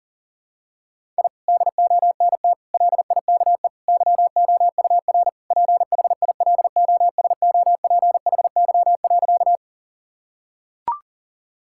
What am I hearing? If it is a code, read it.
Morse code, heard as a keyed tone, about 40 wpm, 700 Hz. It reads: I DONT LIKE YOUR PHILOSOPHY.
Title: Sentences from Top 1000 Words - Review Part 6 - 40wpm